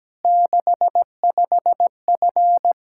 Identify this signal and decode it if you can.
Morse code, heard as a keyed tone, about 17 wpm, 700 Hz. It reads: 65F
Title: Practice QSO - Exchange 2 - Info - 17wpm